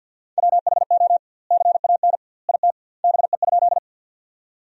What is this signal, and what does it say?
Morse code, heard as a keyed tone, about 50 wpm, 700 Hz. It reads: WHO CAN IT BE?
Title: 2x Word-Spacing - Sentences from Top 100 Words - 50wpm